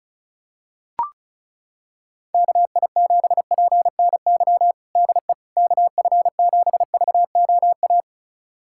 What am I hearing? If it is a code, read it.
Morse code, heard as a keyed tone, about 34 wpm, 700 Hz. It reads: KI7PNY DE KF7VOA
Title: USA Callsign DE Callsign - 6 Characters - 35wpm